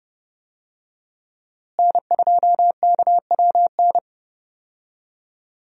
Morse code, heard as a keyed tone, about 32 wpm, 700 Hz. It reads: N2KWN
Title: USA Call Signs - 5 Characters - Repeated 3x - 30wpm